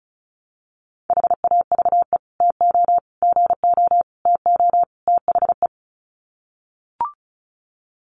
Morse code, heard as a keyed tone, about 34 wpm, 700 Hz. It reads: HAVE TO GO TO THE
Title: Sets of 5 Words Encore 35wpm